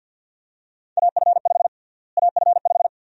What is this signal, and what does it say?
Morse code, heard as a keyed tone, about 47 wpm, 700 Hz. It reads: AF5 AF5
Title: USA Call Sign Prefixes - Repeated 2x - 50wpm